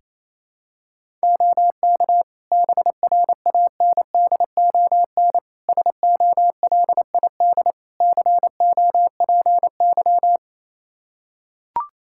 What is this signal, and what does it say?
Morse code, heard as a keyed tone, about 28 wpm, 700 Hz. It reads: OK BRANDON SOLID COPY
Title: K1USN SST QSO Practice 28wpm